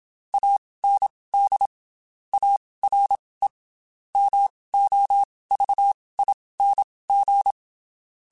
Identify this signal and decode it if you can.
Morse code, heard as a keyed tone, about 27 wpm, spaced as for 19 wpm, 800 Hz. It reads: AND ARE MOVING
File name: QSOs / QSOs-mp3s / WD0FBT 1@25